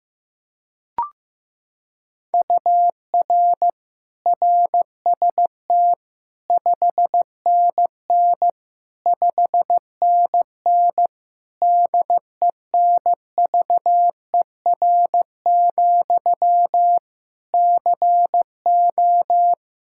Morse code, heard as a keyed tone, about 15 wpm, 700 Hz. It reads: UR RST 5NN 5NN DENVER, CO
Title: Practice QSO - Exchange 1 - Beginning - 15wpm